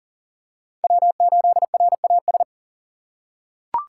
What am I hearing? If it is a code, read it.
Morse code, heard as a keyed tone, about 40 wpm, 690 Hz. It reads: W8RAS
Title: USA Call Signs - 5 Characters - 6x Character Spacing - 40wpm